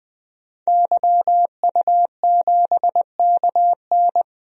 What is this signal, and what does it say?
Morse code, heard as a keyed tone, about 20 wpm, 700 Hz. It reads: YU7KN
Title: DXCC-100 Call Signs - 5 Characters - 20wpm